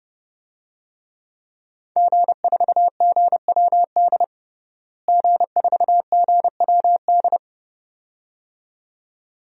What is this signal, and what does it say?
Morse code, heard as a keyed tone, about 30 wpm, 700 Hz. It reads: G4GWD G4GWD
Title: DXCC-100 Call Signs - 5 Characters - Repeated 2x - 30wpm